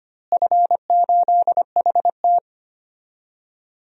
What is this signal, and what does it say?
Morse code, heard as a keyed tone, about 25 wpm, 700 Hz. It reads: F8HT
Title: DXCC-100 Call Signs - 4 Characters - Repeated 3x - 25wpm